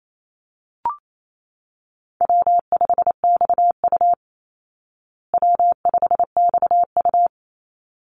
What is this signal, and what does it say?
Morse code, heard as a keyed tone, about 28 wpm, 700 Hz. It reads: W5XU W5XU
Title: Easy Call Signs Repeated 2x - 28wpm